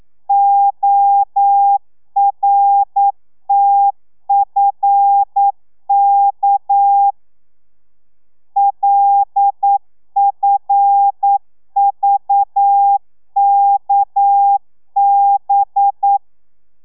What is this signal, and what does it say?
Morse code, heard as a keyed tone, about 9 wpm, 800 Hz. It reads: ORTFK LFVKB
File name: telegrafia / i2tfj / Lez6 40